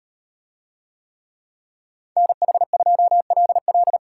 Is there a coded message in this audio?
Yes: Morse code from 2.2 to 4.0 s, heard as a keyed tone, about 36 wpm, 700 Hz. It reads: NH2LL